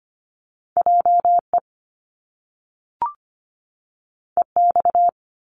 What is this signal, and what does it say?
Morse code, heard as a keyed tone, about 25 wpm, 700 Hz. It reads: JE EX
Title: Top Two Letter Combinations in Words 25wpm